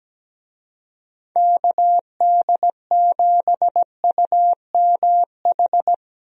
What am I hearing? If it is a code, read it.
Morse code, heard as a keyed tone, about 17 wpm, 700 Hz. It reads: KD7UMH
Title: USA Call Signs - 6 Characters - 17wpm